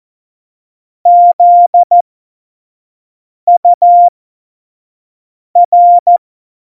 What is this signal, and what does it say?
Morse code, heard as a keyed tone, about 14 wpm, spaced as for 5 wpm, 700 Hz. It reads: ZUR
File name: Morsekurs / Witze / Witz 28 Juli 2020, 25 F70